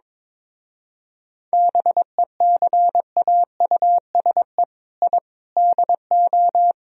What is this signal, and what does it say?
Morse code, heard as a keyed tone, about 22 wpm, 700 Hz. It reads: BECAUSE I DO
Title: Speed-Racing - Sets of 4 Words 22wpm